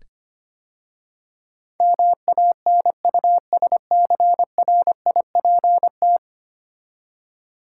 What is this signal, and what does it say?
Morse code, heard as a keyed tone, about 25 wpm, 690 Hz. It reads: MANUSCRIPT
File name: 10 Letter Words - First 3 Letters Given - 25wpm